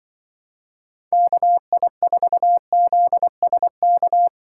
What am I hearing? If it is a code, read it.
Morse code, heard as a keyed tone, about 25 wpm, 700 Hz. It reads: KI4ZSK
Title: Speed Racing - USA 2x3 Call Signs 20wpm